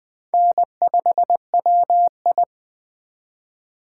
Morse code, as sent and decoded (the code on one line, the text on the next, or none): -. ..... .-- ..
N5WI